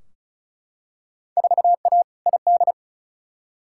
Morse code, heard as a keyed tone, about 36 wpm, 690 Hz. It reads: 4A ID
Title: Field Day Exchanges Encore - 35wpm